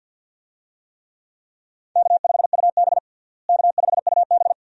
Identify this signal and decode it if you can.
Morse code, heard as a keyed tone, about 50 wpm, 700 Hz. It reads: K5UB K5UB
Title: USA Call Signs - 4 Characters - Repeated 2x - 50wpm